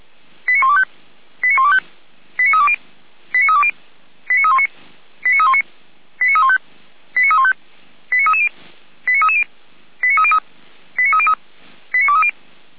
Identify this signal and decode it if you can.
Five-tone selective call (ZVEI), heard as quick runs of five tones, tones of about 70 ms. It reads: 89125 89125 89320 89320 89319 89319 89315 89315 89330 89330 89303 89303 89230